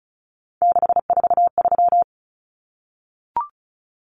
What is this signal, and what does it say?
Morse code, heard as a keyed tone, about 34 wpm, 700 Hz. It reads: 643
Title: Common Cut Numbers - 3-Digit 35wpm